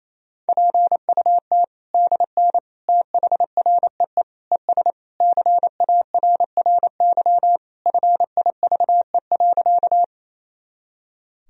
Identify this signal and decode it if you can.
Morse code, heard as a keyed tone, about 28 wpm, 700 Hz. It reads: PUT DN THREE ES CARRY FIVE.
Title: Sentences from Top 400 Words and Abbreviations 28wpm